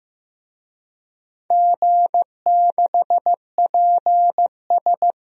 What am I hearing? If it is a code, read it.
Morse code, heard as a keyed tone, about 15 wpm, 700 Hz. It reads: G6PS
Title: DXCC-100 Call Signs - 4 Characters - Repeated 2x - 15wpm